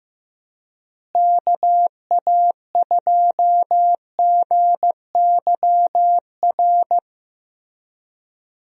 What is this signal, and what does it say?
Morse code, heard as a keyed tone, about 15 wpm, 700 Hz. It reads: KA2GYR
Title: USA Call Signs - 6 Characters - Repeated 3x - 15wpm